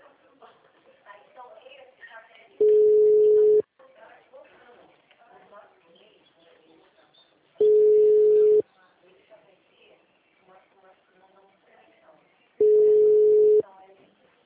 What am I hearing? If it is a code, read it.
Ringback tone (Europe, 425 Hz), with speech.